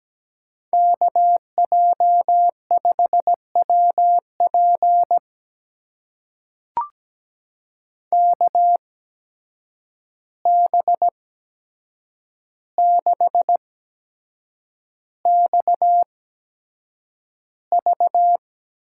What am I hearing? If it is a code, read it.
Morse code, heard as a keyed tone, about 17 wpm, spaced as for 7 wpm, 700 Hz. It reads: KJ5WP KB6XV
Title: USA Call Signs - 5 Characters - 8x Character Spacing - 17wpm